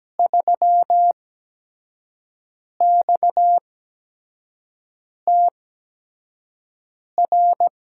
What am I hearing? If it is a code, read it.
Morse code, heard as a keyed tone, about 17 wpm, spaced as for 5 wpm, 700 Hz. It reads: 3XTR
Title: USA Call Signs - 5 Characters - 8x Character Spacing - 17wpm